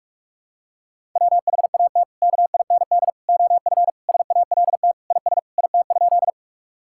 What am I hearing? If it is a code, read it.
Morse code, heard as a keyed tone, about 46 wpm, 700 Hz. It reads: WHAT KIND OF SALT IS IT?
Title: Sentences from Top 1000 Words - Review Part 5 - 45wpm